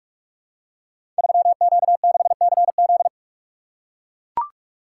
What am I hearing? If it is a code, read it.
Morse code, heard as a keyed tone, about 44 wpm, 700 Hz. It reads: 3Q6CZ
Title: Code Groups - 5 Letter:Numbers - 45wpm